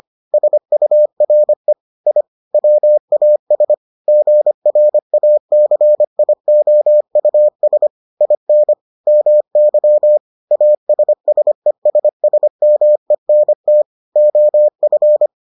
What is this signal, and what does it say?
Morse code, heard as a keyed tone, about 25 wpm, 600 Hz. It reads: SURE I WAS GRACIOUS IN MY ASSESSMENT OF